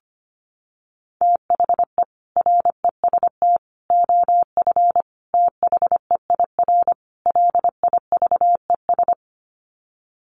Morse code, heard as a keyed tone, about 25 wpm, 700 Hz. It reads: THE REST OF THEIR LIVES
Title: Sets of 5 Words Encore 25wpm